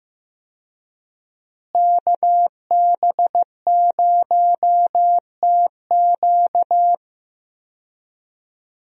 Morse code, heard as a keyed tone, about 15 wpm, 700 Hz. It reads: KB0TQ
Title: USA Call Signs - 5 Characters - Repeated 3x - 15wpm